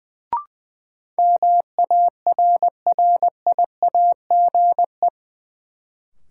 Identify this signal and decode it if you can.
Morse code, heard as a keyed tone, about 20 wpm, 700 Hz. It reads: MARRIAGE
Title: Sentences from Top 800 Words - New Word Drill - 20wpm